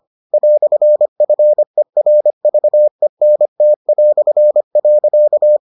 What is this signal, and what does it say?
Morse code, heard as a keyed tone, about 25 wpm, 600 Hz. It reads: "FERVENT".